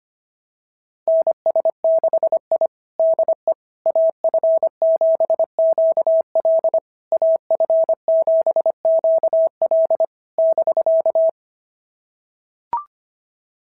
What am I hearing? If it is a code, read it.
Morse code, heard as a keyed tone, about 25 wpm, 690 Hz. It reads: NS6I DE AF7QL AF7QL <BK>
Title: SOTA QSO Practice 25wpm